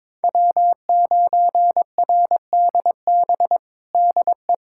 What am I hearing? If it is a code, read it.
Morse code, heard as a keyed tone, about 22 wpm, 700 Hz. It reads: W9RDB DE
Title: USA Callsign DE Callsign - 5 Characters - 22wpm